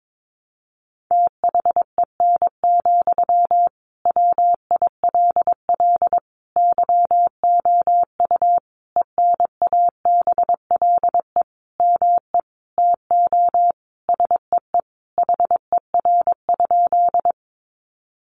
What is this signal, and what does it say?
Morse code, heard as a keyed tone, about 22 wpm, 700 Hz. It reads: THEN, WILL YOU ENABLE ME TO SEE HER?
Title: Sentences from Top 1500 Words 22wpm